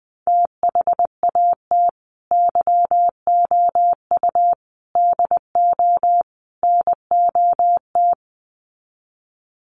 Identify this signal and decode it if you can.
Morse code, heard as a keyed tone, about 20 wpm, 700 Hz. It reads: THAT YOU DO NOT
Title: Sets of 4 Words 20wpm